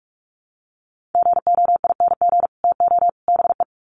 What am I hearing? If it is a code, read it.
Morse code, heard as a keyed tone, about 44 wpm, 700 Hz. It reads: GOING TO BE
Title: Sets of 3 Words 45wpm